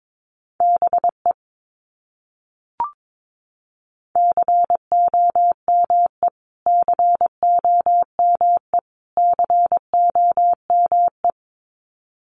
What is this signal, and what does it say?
Morse code, heard as a keyed tone, about 22 wpm, 700 Hz. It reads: BE COME COME COME
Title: Top 200 Words Repeated 22wpm